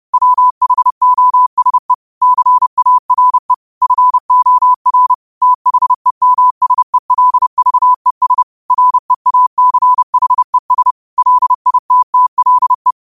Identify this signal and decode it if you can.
Morse code, heard as a keyed tone, about 30 wpm, 1000 Hz. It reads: WHOSE CARE FOR THEMSELVES REACHES LITTLE
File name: qotdfiles / 30wpm